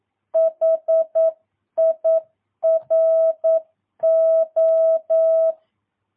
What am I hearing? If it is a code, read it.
Morse code, heard as a keyed tone, about 9 wpm, 650 Hz. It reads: HIRO
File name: jh8cbh / 2022cw